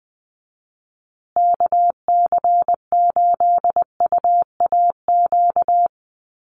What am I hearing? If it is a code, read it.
Morse code, heard as a keyed tone, about 20 wpm, 700 Hz. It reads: KC8UAQ
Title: USA Call Signs Repeated 3x 20wpm